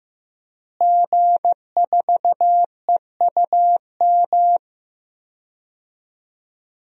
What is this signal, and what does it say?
Morse code, heard as a keyed tone, about 16 wpm, 700 Hz. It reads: G4EUM
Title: DXCC-100 Call Signs - 5 Characters - Repeated 3x - 15wpm